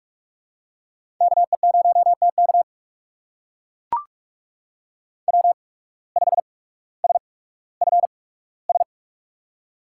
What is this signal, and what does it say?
Morse code, heard as a keyed tone, about 44 wpm, spaced as for 19 wpm, 690 Hz. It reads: KE0TK W5SFS
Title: USA Call Signs - 5 Characters - 8x Character Spacing - 45wpm